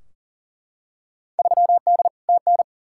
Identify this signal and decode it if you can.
Morse code, heard as a keyed tone, about 40 wpm, 700 Hz. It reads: 3D TN
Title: Field Day Exchanges Encore - 40wpm